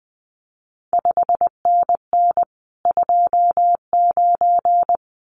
Morse code, heard as a keyed tone, about 19 wpm, 700 Hz. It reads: 5NN 29
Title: CQ WW DX Contest Exchange Encore - 20wpm